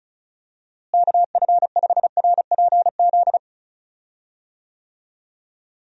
Morse code, heard as a keyed tone, about 36 wpm, 700 Hz. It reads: KF5RPZ
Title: USA Call Signs - 6 Characters - Repeated 2x - 35wpm